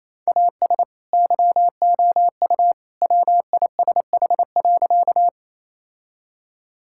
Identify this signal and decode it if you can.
Morse code, heard as a keyed tone, about 28 wpm, 700 Hz. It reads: AS YOU WISH.
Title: Sentences from Top 1600 Words 28wpm